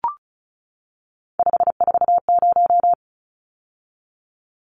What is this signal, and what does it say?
Morse code, heard as a keyed tone, about 34 wpm, 700 Hz. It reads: 540
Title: Numbers - 3-Digit 35wpm